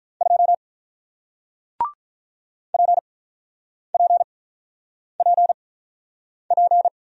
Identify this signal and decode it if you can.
Morse code, heard as a keyed tone, about 48 wpm, spaced as for 9 wpm, 700 Hz. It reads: 2 PPPP